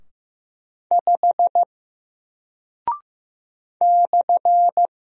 Morse code, heard as a keyed tone, about 15 wpm, 700 Hz. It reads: 5 /
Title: Single Character (Letter:Number:Punctuation:Prosign) Alt 15wpm